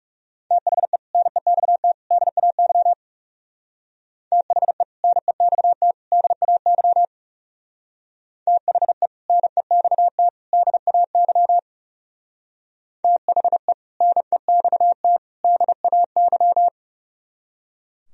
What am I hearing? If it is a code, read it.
Morse code, heard as a keyed tone, about 37 wpm, 700 Hz. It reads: THE NEXT DAY THE NEXT DAY THE NEXT DAY THE NEXT DAY
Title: Speed-Racing - Sets of 3 Words 30wpm